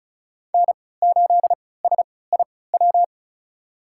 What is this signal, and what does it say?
Morse code, heard as a keyed tone, about 34 wpm, spaced as for 20 wpm, 700 Hz. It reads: N8SIW
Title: USA Call Signs - 5 Characters - 3x Character Spacing - 35wpm